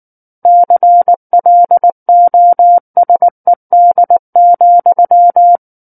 Morse code, heard as a keyed tone, about 19 wpm, 700 Hz. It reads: CLOSED,
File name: Book One / 10-Chapter 10 - 19 WPM